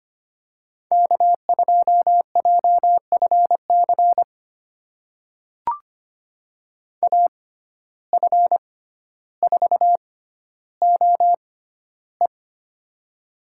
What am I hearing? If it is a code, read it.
Morse code, heard as a keyed tone, about 25 wpm, spaced as for 13 wpm, 700 Hz. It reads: K2JFC AF4OE
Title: USA Call Signs - 5 Characters - 6x Character Spacing - 25wpm